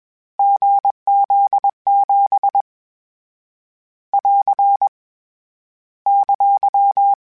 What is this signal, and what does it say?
Morse code, heard as a keyed tone, about 21 wpm, 800 Hz. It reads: GZ7 <AR> !
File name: Difficult Combos / Difficult Combos-mp3s / GZ GZ7 @20